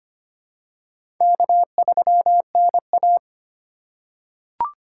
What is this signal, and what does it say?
Morse code, heard as a keyed tone, about 25 wpm, 700 Hz. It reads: K3NA
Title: USA Call Signs - 4 Characters - 2x Character Spacing - 25wpm